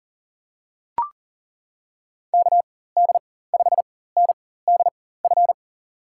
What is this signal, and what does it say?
Morse code, heard as a keyed tone, about 40 wpm, spaced as for 19 wpm, 700 Hz. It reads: KD5NDF
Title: USA Call Signs - 6 Characters - 4x Character Spacing - 40wpm